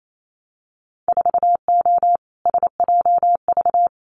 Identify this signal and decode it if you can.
Morse code, heard as a keyed tone, about 28 wpm, 700 Hz. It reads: 4O SJV
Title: Winter Field Day Exchanges Encore 28wpm